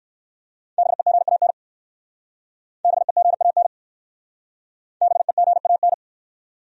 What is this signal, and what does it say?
Morse code, heard as a keyed tone, about 60 wpm, 700 Hz. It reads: BEGAN BEGAN BEGAN